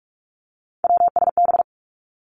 Morse code, heard as a keyed tone, about 46 wpm, 700 Hz. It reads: WSB
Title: Three Challenging Letter-Number Combinations 45wpm